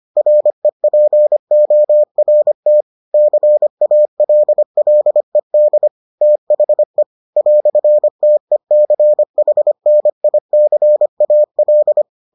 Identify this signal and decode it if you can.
Morse code, heard as a keyed tone, about 25 wpm, 600 Hz. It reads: REPORT CALLED THE "TECHNICAL